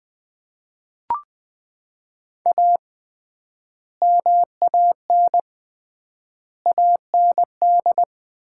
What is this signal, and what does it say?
Morse code, heard as a keyed tone, about 20 wpm, 700 Hz. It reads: A MAN AND
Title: 3x Word-Spacing - Sets of 5 Words - 20wpm